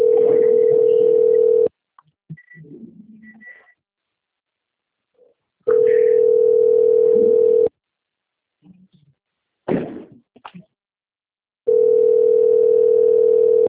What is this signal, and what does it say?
Ringback tone (North America), 440 + 480 Hz.
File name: out- - - - - .28776